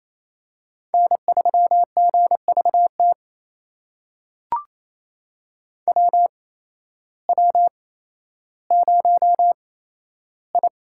Morse code, heard as a keyed tone, about 28 wpm, spaced as for 13 wpm, 700 Hz. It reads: N3GVT WW0I
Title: USA Call Signs - 8x Character Spacing - 28wpm